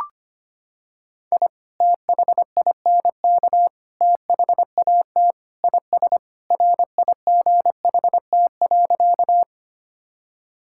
Morse code, heard as a keyed tone, about 25 wpm, 700 Hz. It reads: I THINK THAT IS RIGHT.
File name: practice / 25wpm / Sentences with 3-Word Phrases - 25wpm